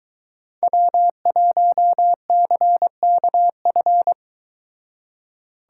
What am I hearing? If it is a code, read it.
Morse code, heard as a keyed tone, about 22 wpm, 700 Hz. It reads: W1CKF